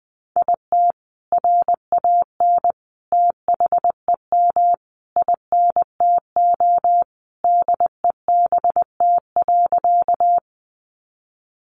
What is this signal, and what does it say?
Morse code, heard as a keyed tone, about 20 wpm, 700 Hz. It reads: IT RAN THEM INTO DEBT.
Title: Sentences from Top 1300 Words 20wpm